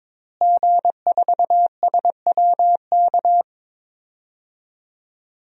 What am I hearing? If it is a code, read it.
Morse code, heard as a keyed tone, about 22 wpm, 700 Hz. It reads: G4SWK
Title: DXCC-100 Call Signs - 5 Characters - Repeated 3x - 22wpm